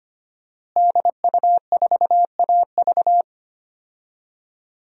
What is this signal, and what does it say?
Morse code, heard as a keyed tone, about 25 wpm, 700 Hz. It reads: DU4AV
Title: DXCC-100 Call Signs - 5 Characters - Repeated 2x - 25wpm